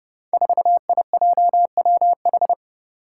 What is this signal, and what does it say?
Morse code, heard as a keyed tone, about 28 wpm, 700 Hz. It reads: 4IJWH
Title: Five Challenging Letter-Number Combinations 30wpm